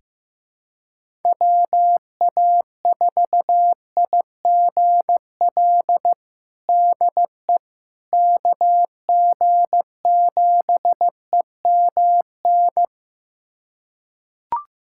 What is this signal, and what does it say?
Morse code, heard as a keyed tone, about 14 wpm, 700 Hz. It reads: WA4IGL DE KG7EMN
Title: USA Callsign DE Callsign - 6 Characters - 15wpm